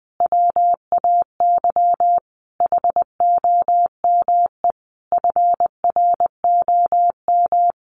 Morse code, heard as a keyed tone, about 20 wpm, 700 Hz. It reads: WAY HOME FROM
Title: Sentences from Top 1400 Words 20wpm